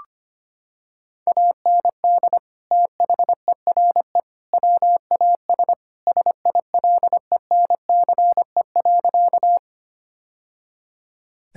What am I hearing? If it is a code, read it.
Morse code, heard as a keyed tone, about 25 wpm, 700 Hz. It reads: AND THERE WAS SILENCE.
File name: practice / 25wpm / Sentences with 3-Word Phrases - 25wpm